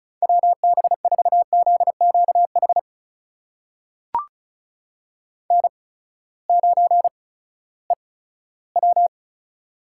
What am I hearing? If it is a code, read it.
Morse code, heard as a keyed tone, about 36 wpm, spaced as for 18 wpm, 700 Hz. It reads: WB4ZQH N9EW
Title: USA Call Signs - 8x Character Spacing - 35wpm